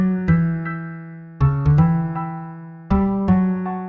There is no code.